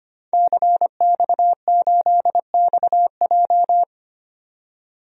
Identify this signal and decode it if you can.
Morse code, heard as a keyed tone, about 25 wpm, 690 Hz. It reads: CX8XJ
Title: DXCC-100 Call Signs - 5 Characters - 25wpm